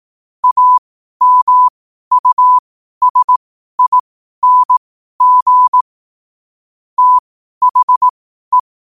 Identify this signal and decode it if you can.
Morse code, heard as a keyed tone, about 18 wpm, spaced as for 12 wpm, 1000 Hz. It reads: AMUSING THE
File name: qotdfiles / 12wpm